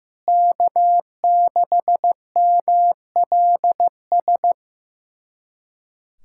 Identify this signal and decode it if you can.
Morse code, heard as a keyed tone, about 16 wpm, 700 Hz. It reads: K6MLS
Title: USA Callsign DE Callsign 15wpm